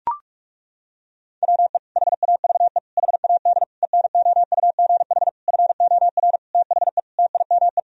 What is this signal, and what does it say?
Morse code, heard as a keyed tone, about 42 wpm, 700 Hz. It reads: WE HAVE HAD ENOUGH FOR THE TIME
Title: Sentences from Top 1000 Words - Review Part 3 - 45wpm